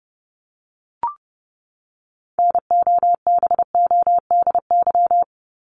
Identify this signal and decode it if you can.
Morse code, heard as a keyed tone, about 30 wpm, 700 Hz. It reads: NOBODY
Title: Top 1501-1750 Words 30wpm